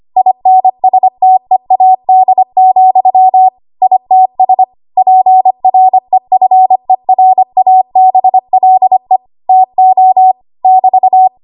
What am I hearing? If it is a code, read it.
Morse code, heard as a keyed tone, about 25 wpm, 750 Hz. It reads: INSTEAD, ITS PREFERABLE TO <BT>
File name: Archive / 25 WPM / 25WPM